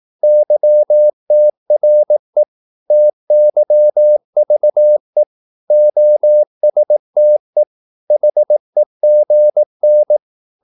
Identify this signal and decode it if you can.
Morse code, heard as a keyed tone, about 18 wpm, 600 Hz. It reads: YTRE TYVE OSTE HEGN
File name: Korte ord DK 18wpm